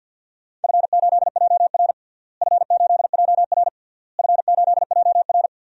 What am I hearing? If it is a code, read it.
Morse code, heard as a keyed tone, about 50 wpm, 700 Hz. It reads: F8JR F8JR F8JR
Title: DXCC-100 Call Signs - 4 Characters - Repeated 3x - 50wpm